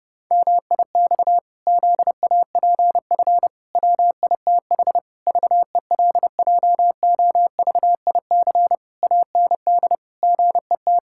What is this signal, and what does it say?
Morse code, heard as a keyed tone, about 30 wpm, 700 Hz. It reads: MIX ZAPF WITH VELJOVIC AND GET
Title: Pangram Challenge v2 - 30wpm